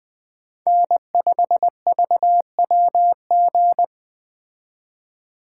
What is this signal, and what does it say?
Morse code, heard as a keyed tone, about 20 wpm, 700 Hz. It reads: N5VWG
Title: Speed Racing - USA Call Signs 17wpm